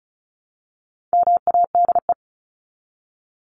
Morse code, heard as a keyed tone, about 34 wpm, 700 Hz. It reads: MADE
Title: Top 300 Words 35wpm